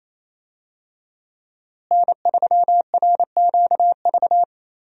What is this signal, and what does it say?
Morse code, heard as a keyed tone, about 27 wpm, 700 Hz. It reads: N3RQV
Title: USA Call Signs - 5 Characters - Repeated 3x - 28wpm